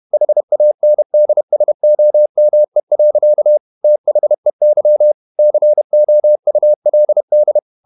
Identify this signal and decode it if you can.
Morse code, heard as a keyed tone, about 31 wpm, 600 Hz. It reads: HANDSOME. THEY COULD
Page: Never 31wpm | CW med Gnister